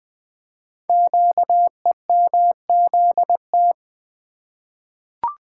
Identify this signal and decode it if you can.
Morse code, heard as a keyed tone, about 20 wpm, 700 Hz. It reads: QEMZT
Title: Code Groups - 5 Letters - 20wpm